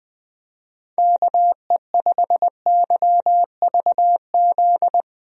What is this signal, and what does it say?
Morse code, heard as a keyed tone, about 20 wpm, 700 Hz. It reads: KE5YVZ
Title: USA Call Signs - 6 Characters - Repeated 3x - 20wpm